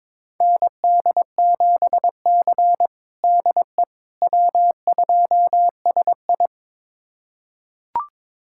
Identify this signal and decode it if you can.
Morse code, heard as a keyed tone, about 22 wpm, 700 Hz. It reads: ND7C DE W2SI
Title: USA Callsign DE Callsign - 4 Characters - 22wpm